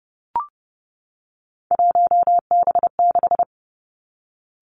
Morse code, heard as a keyed tone, about 30 wpm, 700 Hz. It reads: 1B6
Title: Three Challenging Letter-Number Combinations 30wpm